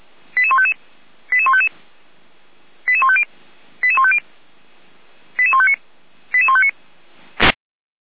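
Five-tone selective call (ZVEI), heard as quick runs of five tones, tones of about 70 ms. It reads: 88155 88155 88150 88150 88159 88159
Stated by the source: ZVEI five-tone selective call received on an SDR.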